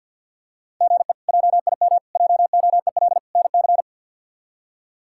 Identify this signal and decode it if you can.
Morse code, heard as a keyed tone, about 53 wpm, 700 Hz. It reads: GE JIM JOEL NC
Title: K1USN SST QSO Practice - 50wpm